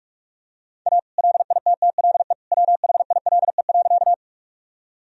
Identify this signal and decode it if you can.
Morse code, heard as a keyed tone, about 46 wpm, 700 Hz. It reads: A LITTLE WHILE.
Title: Sentences from Top 1000 Words - Review Part 1 - 45wpm